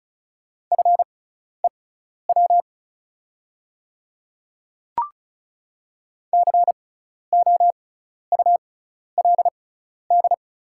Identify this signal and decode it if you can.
Morse code, heard as a keyed tone, about 36 wpm, spaced as for 12 wpm, 690 Hz. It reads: FEW COULD
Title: 6x Character Spacing - Top 500 Words - 35wpm